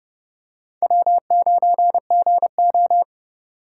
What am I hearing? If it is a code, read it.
Morse code, heard as a keyed tone, about 31 wpm, 700 Hz. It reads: W9GO